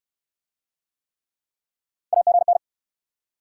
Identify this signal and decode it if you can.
Morse code, heard as a keyed tone, about 69 wpm, 700 Hz. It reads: AGN